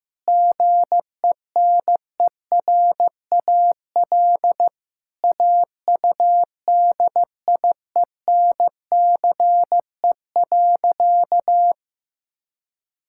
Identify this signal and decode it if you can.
Morse code, heard as a keyed tone, about 15 wpm, 700 Hz. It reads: GENERAL AUDIENCE.
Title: Sentences from Top 1000 Words - Review Part 5 - 15wpm